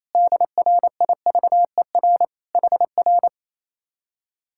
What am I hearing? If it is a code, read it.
Morse code, heard as a keyed tone, about 26 wpm, 700 Hz. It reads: DRIVER HR
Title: Speed-Racing Common QSO Phrases - 28wpm